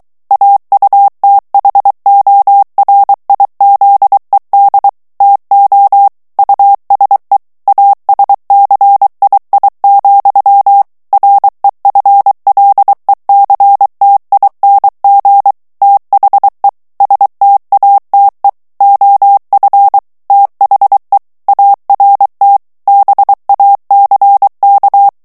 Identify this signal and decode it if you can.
Morse code, heard as a keyed tone, about 23 wpm, 800 Hz. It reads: AUTHORIZED TO USE ASCII, REFLECTING THE STATE OF THE ART BACK
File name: ANGLAIS texte clair 3 vitesse 22 mots minute